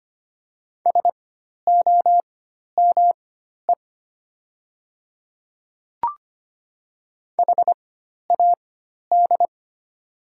Morse code, heard as a keyed tone, about 25 wpm, spaced as for 12 wpm, 700 Hz. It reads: SOME HAD
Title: 4x Character Spacing - Top 100 Words - 25wpm